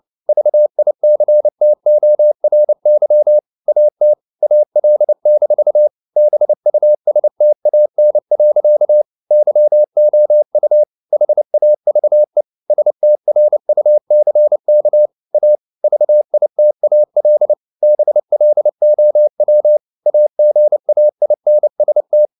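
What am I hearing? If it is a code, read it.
Morse code, heard as a keyed tone, about 29 wpm, 600 Hz. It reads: VICTORY AT AL-BUSTAN. YOU HAVE STRUCK A VITAL BLOW AGAINST
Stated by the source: Morse code at 29 wpm, sending Ken Follett's novel Never in English, from CW med Gnister.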